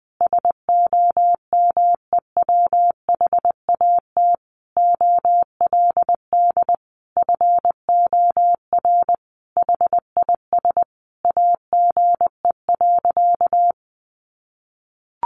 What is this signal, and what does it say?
Morse code, heard as a keyed tone, about 20 wpm, 700 Hz. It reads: SOMEWHAT OLD FOR HIS AGE.
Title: Sentences from Top 1300 Words 20wpm